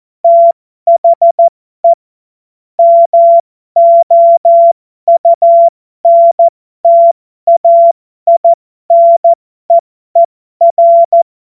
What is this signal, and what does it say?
Morse code, heard as a keyed tone, about 14 wpm, spaced as for 12 wpm, 680 Hz. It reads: THE MOUNTAINEER